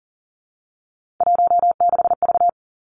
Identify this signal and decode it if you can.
Morse code, heard as a keyed tone, about 41 wpm, 700 Hz. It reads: 16V
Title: Three Challenging Letter-Number Combinations 40wpm